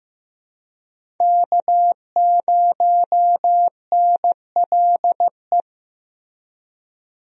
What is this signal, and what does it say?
Morse code, heard as a keyed tone, about 15 wpm, 700 Hz. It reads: K0NLE
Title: USA Call Signs - 5 Characters - Repeated 3x - 15wpm